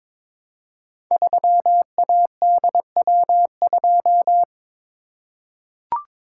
Morse code, heard as a keyed tone, about 22 wpm, 700 Hz. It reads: 3ADW2
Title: Code Groups - 5 Letter:Numbers - 22wpm